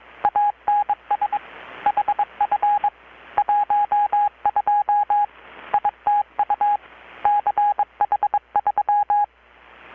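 Morse code, heard as a keyed tone, about 22 wpm, 820 Hz. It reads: ANS HF 12 ITU CH3
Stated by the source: Morse from KPH's Night of Nights XX maritime radiotelegraphy event, received in Chiapas, Mexico about 3,800 km away on a Kenwood R-600 with a grounded 10 m long-wire antenna.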